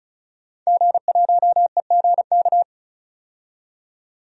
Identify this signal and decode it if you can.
Morse code, heard as a keyed tone, about 34 wpm, 700 Hz. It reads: G1EGK
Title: DXCC-100 Call Signs - 5 Characters - 35wpm